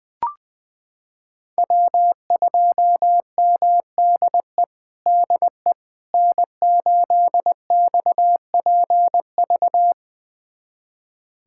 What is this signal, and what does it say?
Morse code, heard as a keyed tone, about 20 wpm, 700 Hz. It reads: W2MDE DE N8XPV
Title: USA Callsign DE Callsign - 5 Characters - 20wpm